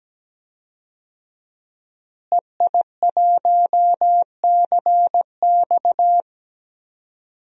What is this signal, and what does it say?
Morse code, heard as a keyed tone, about 17 wpm, 700 Hz. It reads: EI1CX
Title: DXCC-100 Call Signs - 5 Characters - Repeated 2x - 17wpm